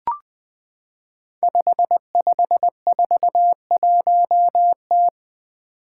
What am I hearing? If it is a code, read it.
Morse code, heard as a keyed tone, about 20 wpm, 700 Hz. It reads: 5541T
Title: Five Challenging Letter-Number Combinations 20wpm